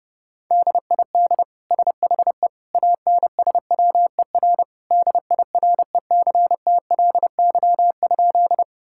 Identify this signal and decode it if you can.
Morse code, heard as a keyed tone, about 31 wpm, 700 Hz. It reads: DID SHE ANSWER DIRECTLY?
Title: Sentences from Top 1000 Words - Review Part 5 - 30wpm